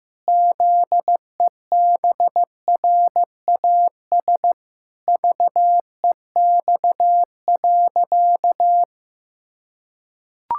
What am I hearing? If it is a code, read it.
Morse code, heard as a keyed tone, about 15 wpm, 700 Hz. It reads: ZEBRAS VEX.
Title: Pangram Challenge v2 - 15wpm